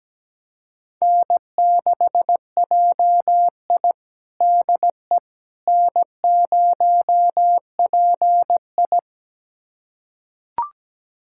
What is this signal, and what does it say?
Morse code, heard as a keyed tone, about 17 wpm, 700 Hz. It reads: N6JI DE N0PI
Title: USA Callsign DE Callsign - 4 Characters - 17wpm